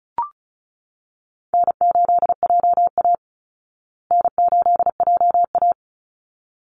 Morse code, heard as a keyed tone, about 34 wpm, 700 Hz. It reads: N8JA N8JA
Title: Easy Call Signs Repeated 2x - 35wpm